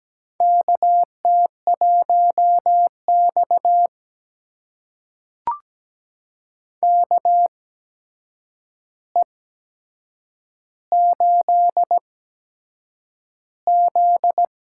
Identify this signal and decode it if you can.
Morse code, heard as a keyed tone, about 17 wpm, spaced as for 7 wpm, 700 Hz. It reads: KT1X KE8Z
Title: USA Call Signs - 4 Characters - 8x Character Spacing - 17wpm